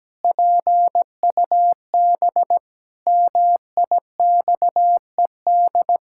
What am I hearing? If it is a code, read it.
Morse code, heard as a keyed tone, about 17 wpm, 700 Hz. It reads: PUB MIXED
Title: Pangram Challenge v2 - 17wpm